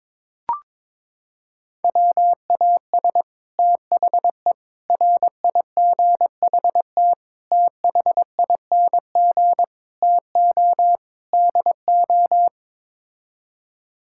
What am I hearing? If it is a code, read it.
Morse code, heard as a keyed tone, about 22 wpm, 700 Hz. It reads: WAS THE RIGHT THING TO DO
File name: practice / 22wpm / Sets of 6 Words - 22wpm